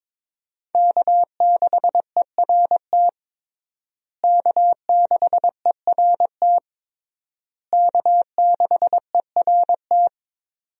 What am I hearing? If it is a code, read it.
Morse code, heard as a keyed tone, about 22 wpm, 700 Hz. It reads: K6ERT K6ERT K6ERT
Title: USA Call Signs - 5 Characters - Repeated 3x - 22wpm